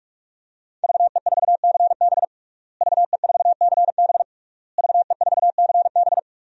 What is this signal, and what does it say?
Morse code, heard as a keyed tone, about 44 wpm, 700 Hz. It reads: VE4CB VE4CB VE4CB
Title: DXCC-100 Call Signs - 5 Characters - Repeated 3x - 45wpm